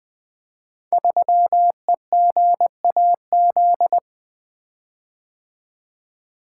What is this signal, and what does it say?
Morse code, heard as a keyed tone, about 20 wpm, 700 Hz. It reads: 3EGAZ
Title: Code Groups - 5 Letter:Numbers - 20wpm